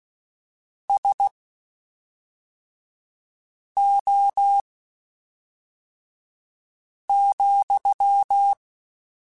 Morse code, heard as a keyed tone, about 16 wpm, spaced as for 3 wpm, 800 Hz. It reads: SO,
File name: QSOs / QSOs-mp3s / WD0FBT 1@5